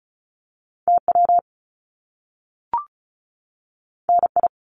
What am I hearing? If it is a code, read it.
Morse code, heard as a keyed tone, about 36 wpm, 700 Hz. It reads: TW NI
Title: Top Two Letter Combinations in Words 35wpm